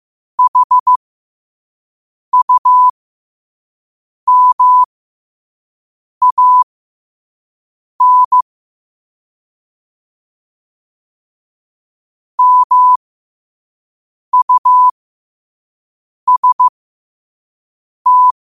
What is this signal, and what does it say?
Morse code, heard as a keyed tone, about 15 wpm, spaced as for 5 wpm, 1000 Hz. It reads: HUMAN MUST
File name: qotdfiles / 5wpm